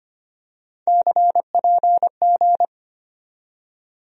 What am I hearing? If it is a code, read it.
Morse code, heard as a keyed tone, about 25 wpm, 700 Hz. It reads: CPG
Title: USA Call Sign Postfixes - 3 Letters 25wpm